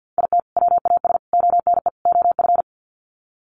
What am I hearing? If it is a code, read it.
Morse code, heard as a keyed tone, about 48 wpm, 700 Hz. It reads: IT WAS ONE OF
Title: Sets of 4 Words 50wpm